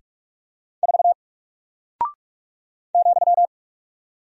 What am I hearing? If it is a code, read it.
Morse code, heard as a keyed tone, about 47 wpm, 700 Hz. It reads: 4 ,
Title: Single Character (Letter:Number:Punctuation:Prosign) Alt 45wpm